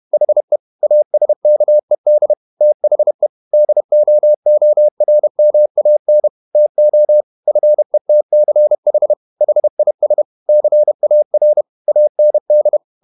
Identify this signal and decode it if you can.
Morse code, heard as a keyed tone, about 31 wpm, 600 Hz. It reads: HE ASKED THE DOORMAN TO FETCH HIS CAR AND
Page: Never 31wpm | CW med Gnister